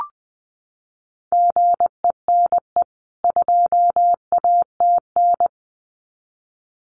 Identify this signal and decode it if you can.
Morse code, heard as a keyed tone, about 20 wpm, 700 Hz. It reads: GENE 2ATN
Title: CWT Exchange 20wpm